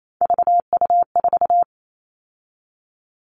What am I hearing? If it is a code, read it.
Morse code, heard as a keyed tone, about 28 wpm, 700 Hz. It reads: VU4
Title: All Cut Numbers - 3-Digit 28wpm